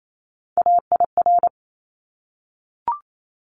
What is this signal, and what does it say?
Morse code, heard as a keyed tone, about 28 wpm, 700 Hz. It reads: AIR
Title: Top 200 Words Repeated 28wpm